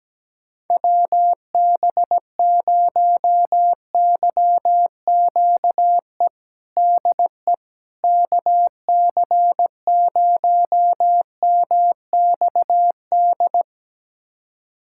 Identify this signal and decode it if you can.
Morse code, heard as a keyed tone, about 17 wpm, 700 Hz. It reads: WB0YQE DE KC0MXD
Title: USA Callsign DE Callsign - 6 Characters - 17wpm